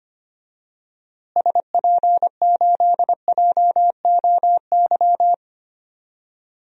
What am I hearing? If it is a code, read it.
Morse code, heard as a keyed tone, about 25 wpm, 700 Hz. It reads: SP8JOY
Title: DXCC-100 Call Signs - 6 Characters - Repeated 2x - 25wpm